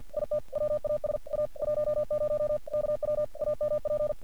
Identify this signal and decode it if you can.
Morse code, heard as a keyed tone, about 48 wpm, 600 Hz. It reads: IT WAS A 10 KW AMP
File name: archived / CRSnet